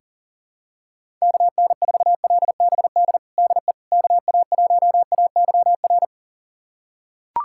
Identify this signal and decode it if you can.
Morse code, heard as a keyed tone, about 40 wpm, 700 Hz. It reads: KN4LBD DE KA1AYR
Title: USA Callsign DE Callsign - 6 Characters - 40wpm